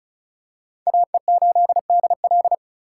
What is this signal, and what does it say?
Morse code, heard as a keyed tone, about 36 wpm, 700 Hz. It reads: AE8DL
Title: USA Call Signs - 4x Character Spacing - 35wpm